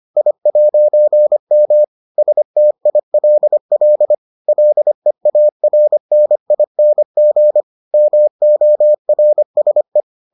Morse code, heard as a keyed tone, about 25 wpm, 600 Hz. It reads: I'M STILL LEARNING MORSE